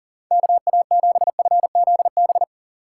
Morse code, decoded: KA7FZB

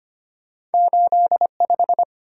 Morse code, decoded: 8 5

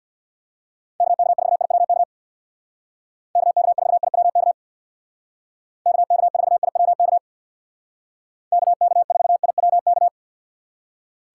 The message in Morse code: KK4IWK KK4IWK KK4IWK KK4IWK

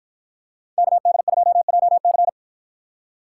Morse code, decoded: KD2JC

